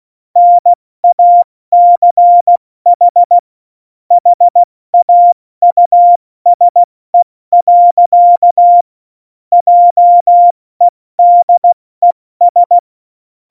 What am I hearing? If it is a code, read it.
Morse code, heard as a keyed tone, about 16 wpm, 700 Hz. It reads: NACH HAUSE. JEDES
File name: Morsekurs / Witze / Witz 21 Juli 2020, 70 F80